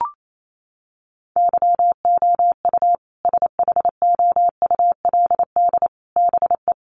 Morse code, heard as a keyed tone, about 28 wpm, 700 Hz. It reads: YOU SHOULD BE
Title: Sets of 5 Words Encore 28wpm